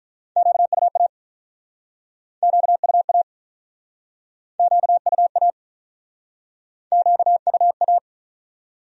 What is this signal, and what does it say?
Morse code, heard as a keyed tone, about 48 wpm, 700 Hz. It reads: QUA QUA QUA QUA